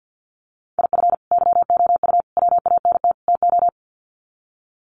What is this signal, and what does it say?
Morse code, heard as a keyed tone, about 48 wpm, 700 Hz. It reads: IF YOU WANT TO
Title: Sets of 4 Words 50wpm